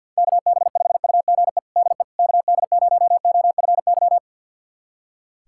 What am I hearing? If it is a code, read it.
Morse code, heard as a keyed tone, about 51 wpm, 690 Hz. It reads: KB5UGE DE KD0OFY
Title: USA Callsign DE Callsign 50wpm